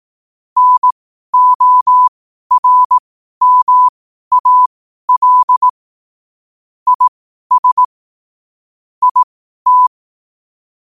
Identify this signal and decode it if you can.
Morse code, heard as a keyed tone, about 18 wpm, spaced as for 12 wpm, 1000 Hz. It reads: NORMAL IS IT
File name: qotdfiles / 12wpm